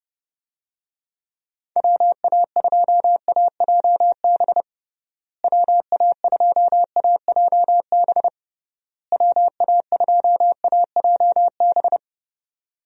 Morse code, heard as a keyed tone, about 29 wpm, 700 Hz. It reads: WA2AJB WA2AJB WA2AJB
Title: USA Call Signs - 6 Characters - Repeated 3x - 30wpm